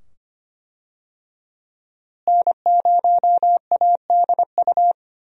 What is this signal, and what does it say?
Morse code, heard as a keyed tone, about 25 wpm, 700 Hz. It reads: N0ADU
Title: USA Call Signs - 5 Characters - Repeated 2x - 25wpm